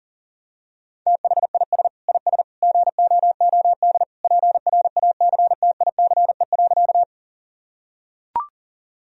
Morse code, heard as a keyed tone, about 40 wpm, 700 Hz. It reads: THIS IS GOOD PRACTICE.
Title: Sentences from Top 1000 Words - Review Part 4 - 40wpm